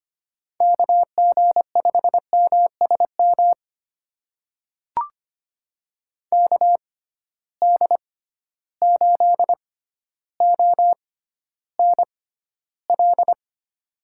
Morse code, heard as a keyed tone, about 25 wpm, spaced as for 13 wpm, 700 Hz. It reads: KG5MSM KD8ONL
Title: USA Call Signs - 6 Characters - 6x Character Spacing - 25wpm